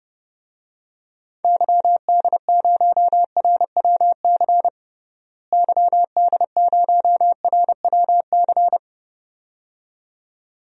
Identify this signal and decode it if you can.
Morse code, heard as a keyed tone, about 30 wpm, 700 Hz. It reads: YD0RWC YD0RWC
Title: DXCC-100 Call Signs - 6 Characters - Repeated 2x - 30wpm